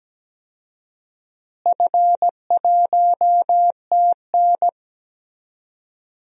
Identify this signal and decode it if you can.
Morse code, heard as a keyed tone, about 17 wpm, 700 Hz. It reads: F1TN
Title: DXCC-100 Call Signs - 4 Characters - Repeated 2x - 17wpm